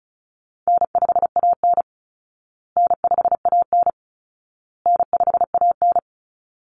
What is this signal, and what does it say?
Morse code, heard as a keyed tone, about 34 wpm, 700 Hz. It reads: N5AN N5AN N5AN
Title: Easy Call Signs Repeated 3x - 35wpm